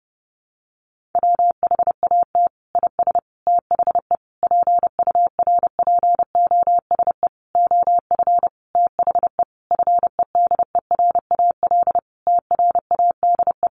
Morse code, heard as a keyed tone, about 30 wpm, 700 Hz. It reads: WHAT IS THE PURPOSE OF THE FEDERAL TRADE
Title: Sentences from Top 1400 Words 30wpm